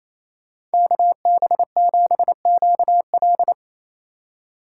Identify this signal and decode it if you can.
Morse code, heard as a keyed tone, about 33 wpm, 700 Hz. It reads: KB7QL
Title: Speed Racing - USA 2x2 Call Signs 28wpm